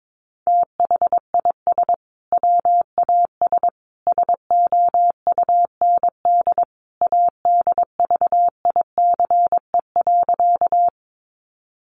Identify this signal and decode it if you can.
Morse code, heard as a keyed tone, about 22 wpm, 700 Hz. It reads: THIS WAS SOUND ADVICE.
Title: Sentences from Top 1100 Words 22wpm